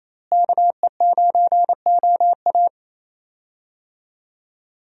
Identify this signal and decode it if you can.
Morse code, heard as a keyed tone, about 27 wpm, 700 Hz. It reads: KE9OA
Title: USA Call Signs - 5 Characters - Repeated 3x - 28wpm